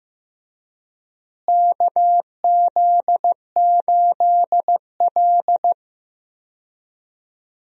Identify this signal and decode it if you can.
Morse code, heard as a keyed tone, about 15 wpm, 700 Hz. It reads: KZ8L
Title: USA Call Signs - 4 Characters - Repeated 3x - 15wpm